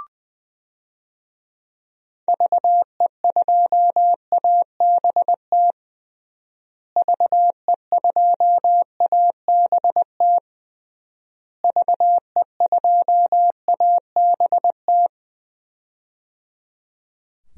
Morse code, heard as a keyed tone, about 20 wpm, 700 Hz. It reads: VE2ABT VE2ABT VE2ABT
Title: DXCC-100 Call Signs - 6 Characters - Repeated 3x - 20wpm